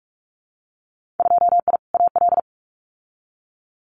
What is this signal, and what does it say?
Morse code, heard as a keyed tone, about 46 wpm, 700 Hz. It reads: 2I AL
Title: Winter Field Day Exchanges 45wpm